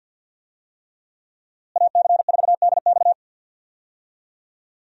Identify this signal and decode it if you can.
Morse code, heard as a keyed tone, about 48 wpm, 700 Hz. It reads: AC4DX
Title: USA Call Signs - 5 Characters - Repeated 3x - 50wpm